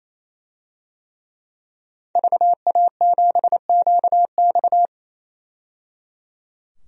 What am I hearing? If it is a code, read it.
Morse code, heard as a keyed tone, about 27 wpm, 700 Hz. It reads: VA7QX
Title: DXCC-100 Call Signs - 5 Characters - 28wpm